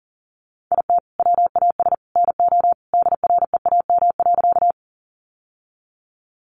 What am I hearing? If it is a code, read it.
Morse code, heard as a keyed tone, about 40 wpm, 700 Hz. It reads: IT WAS NO DREAM.
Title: Sentences from Top 1500 Words 40wpm